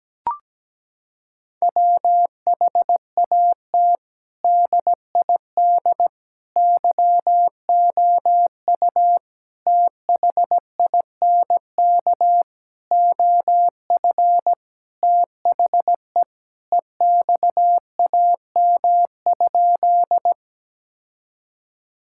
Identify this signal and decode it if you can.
Morse code, heard as a keyed tone, about 17 wpm, 700 Hz. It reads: WHAT DID YOU THINK OF THE EXAM?
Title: Sentences from Top 1000 Words - Review Part 5 - 17wpm